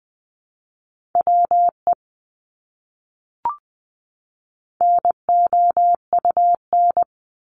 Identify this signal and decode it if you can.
Morse code, heard as a keyed tone, about 20 wpm, 700 Hz. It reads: WE NOUN
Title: Top 500 Words 20wpm